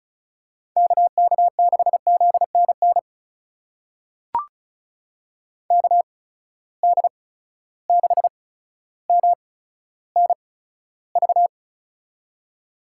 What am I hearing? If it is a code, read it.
Morse code, heard as a keyed tone, about 36 wpm, spaced as for 15 wpm, 700 Hz. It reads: KK6ZNN KD6MNV
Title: USA Call Signs - 6 Characters - 8x Character Spacing - 35wpm